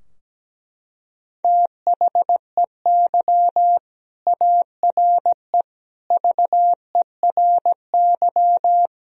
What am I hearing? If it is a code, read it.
Morse code, heard as a keyed tone, about 17 wpm, 700 Hz. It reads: THEY ARE VERY
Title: Sentences from Top 1000 Words - Review Part 4 - 17wpm